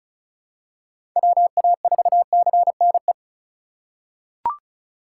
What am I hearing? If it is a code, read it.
Morse code, heard as a keyed tone, about 34 wpm, 700 Hz. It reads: WA4CNE
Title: USA Call Signs - 6 Characters - 2x Character Spacing - 35wpm